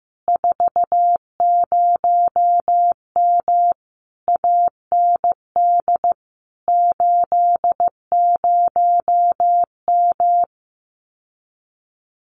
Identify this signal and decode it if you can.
Morse code, heard as a keyed tone, about 15 wpm, 700 Hz. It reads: 40M AND 80M
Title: Common QSO Phrases Encore 15wpm